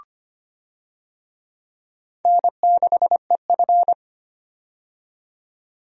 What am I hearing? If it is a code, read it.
Morse code, heard as a keyed tone, about 25 wpm, 700 Hz. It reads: N6EF
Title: USA Call Signs - 4 Characters - 25wpm